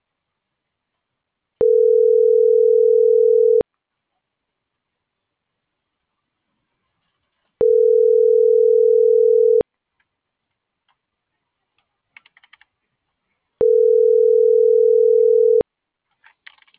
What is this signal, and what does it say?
Ringback tone (North America), 440 + 480 Hz.
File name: CENTRAL SAGOT / out miriam.mercado